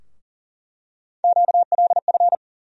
Morse code, heard as a keyed tone, about 40 wpm, 700 Hz. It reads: QLF
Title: USA Call Sign Postfixes - 3 Letters 40wpm